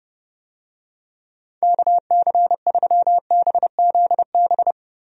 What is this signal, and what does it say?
Morse code, heard as a keyed tone, about 32 wpm, 700 Hz. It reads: KC3BZB
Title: USA Call Signs - 6 Characters - 30wpm